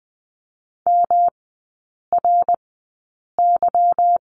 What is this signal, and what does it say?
Morse code, heard as a keyed tone, about 20 wpm, spaced as for 8 wpm, 700 Hz. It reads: MRY
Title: 152 Lesson 25e - Letter p - Three Characters 20wpm